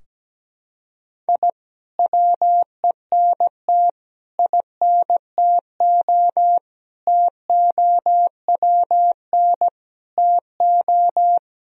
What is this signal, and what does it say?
Morse code, heard as a keyed tone, about 17 wpm, 700 Hz. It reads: I WENT INTO TOWN TO
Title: Sentences from Top 1000 Words - Review Part 4 - 17wpm